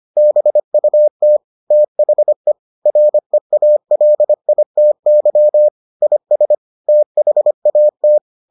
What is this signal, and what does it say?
Morse code, heard as a keyed tone, about 25 wpm, 600 Hz. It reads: BUT THE REALITY IS THAT